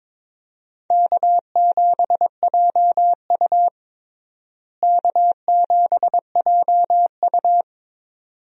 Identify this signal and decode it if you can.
Morse code, heard as a keyed tone, about 22 wpm, 700 Hz. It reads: K7JU K7JU
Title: USA Call Signs - 4 Characters - Repeated 2x - 22wpm